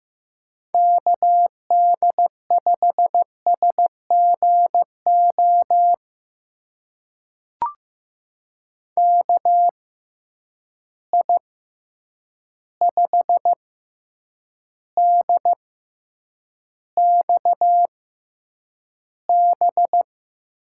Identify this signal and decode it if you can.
Morse code, heard as a keyed tone, about 15 wpm, spaced as for 8 wpm, 700 Hz. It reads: KD5SGO KI5DXB
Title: USA Call Signs - 6 Characters - 6x Character Spacing - 15wpm